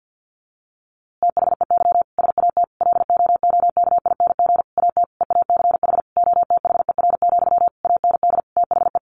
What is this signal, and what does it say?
Morse code, heard as a keyed tone, about 50 wpm, 700 Hz. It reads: THEY SAT LOOKING AT EACH OTHER, AND THE
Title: A Sound Of Thunder 50wpm